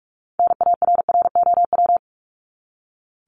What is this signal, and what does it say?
Morse code, heard as a keyed tone, about 45 wpm, 700 Hz. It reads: NARROW
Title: Top 1501-1750 Words 45wpm